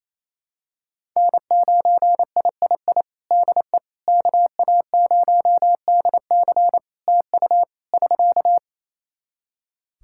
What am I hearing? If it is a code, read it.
Morse code, heard as a keyed tone, about 28 wpm, 700 Hz. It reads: N9III DE KA0DC TU <SK>
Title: Practice QSO - Exchange 4 - Ending - 28wpm